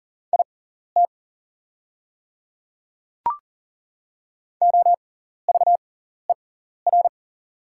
Morse code, heard as a keyed tone, about 40 wpm, spaced as for 14 wpm, 700 Hz. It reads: IT OVER
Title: 6x Character Spacing - Top 100 Words - 40wpm